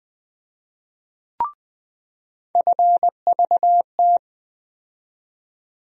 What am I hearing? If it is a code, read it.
Morse code, heard as a keyed tone, about 20 wpm, 700 Hz. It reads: FVT
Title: USA Call Sign Postfixes - 3 Letters 20wpm